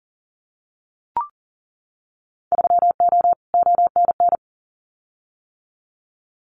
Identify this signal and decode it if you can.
Morse code, heard as a keyed tone, about 40 wpm, 700 Hz. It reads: 3O ONN
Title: Winter Field Day Exchanges 40wpm